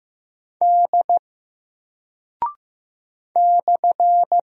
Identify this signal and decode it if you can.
Morse code, heard as a keyed tone, about 15 wpm, 700 Hz. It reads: D /
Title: Single Character (Letter:Number:Punctuation:Prosign) 15wpm